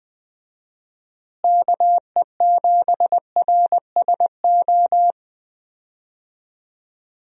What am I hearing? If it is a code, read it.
Morse code, heard as a keyed tone, about 20 wpm, 700 Hz. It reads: KE7RSO
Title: USA Call Signs - 6 Characters - Repeated 3x - 20wpm